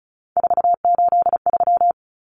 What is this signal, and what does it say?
Morse code, heard as a keyed tone, about 34 wpm, 700 Hz. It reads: 483
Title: Numbers - 3-Digit 35wpm